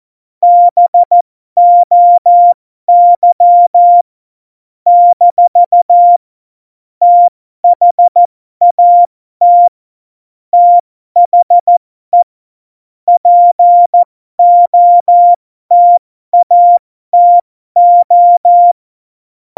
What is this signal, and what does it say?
Morse code, heard as a keyed tone, about 14 wpm, spaced as for 12 wpm, 700 Hz. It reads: BOY - THAT THE POTATO
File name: Morsekurs / Witze